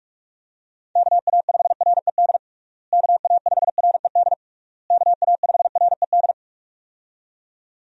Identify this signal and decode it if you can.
Morse code, heard as a keyed tone, about 46 wpm, 700 Hz. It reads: KA5RED KA5RED KA5RED
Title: USA Call Signs - 6 Characters - Repeated 3x - 45wpm